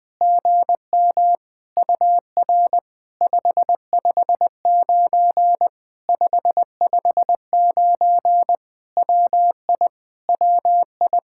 Morse code, heard as a keyed tone, about 20 wpm, 700 Hz. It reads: GM UR 559 559 WI WI
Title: POTA QSO Practice - 20wpm